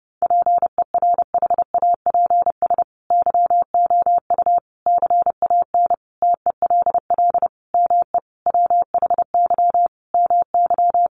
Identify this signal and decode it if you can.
Morse code, heard as a keyed tone, about 30 wpm, 700 Hz. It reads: PERHAPS YOU CAN TELL ME WHY MY
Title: Sentences from Top 1200 Words 30wpm